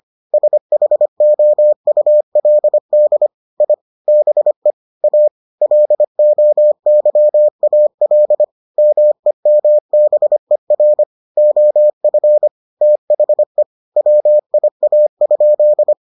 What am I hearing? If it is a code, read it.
Morse code, heard as a keyed tone, about 25 wpm, 600 Hz. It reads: SHOULD I BE A LOYAL MEMBER OF THE WIA?